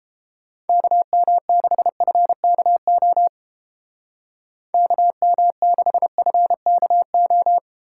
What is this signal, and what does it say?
Morse code, heard as a keyed tone, about 28 wpm, 700 Hz. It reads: KM6FKO KM6FKO